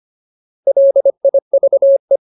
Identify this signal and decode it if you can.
Morse code, heard as a keyed tone, about 25 wpm, 550 Hz. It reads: LIVE
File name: Ultimate Word List / 25WPM Ultimate